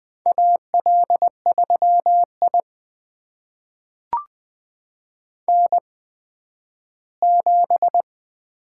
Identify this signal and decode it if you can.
Morse code, heard as a keyed tone, about 20 wpm, spaced as for 12 wpm, 700 Hz. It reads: AL3I N7
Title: USA Call Signs - 4 Characters - 8x Character Spacing - 20wpm